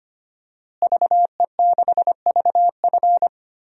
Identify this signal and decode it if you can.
Morse code, heard as a keyed tone, about 25 wpm, 700 Hz. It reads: VE6VF
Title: DXCC-100 Call Signs - 5 Characters - Repeated 2x - 25wpm